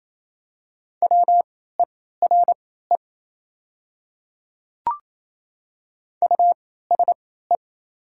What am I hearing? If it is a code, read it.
Morse code, heard as a keyed tone, about 28 wpm, spaced as for 16 wpm, 700 Hz. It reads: WERE USE
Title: 3x Character Spacing - Top 100 Words - 28wpm